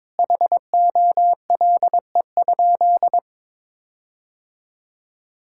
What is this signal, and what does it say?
Morse code, heard as a keyed tone, about 22 wpm, 700 Hz. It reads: HOLE?
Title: Sentences from Top 1600 Words 22wpm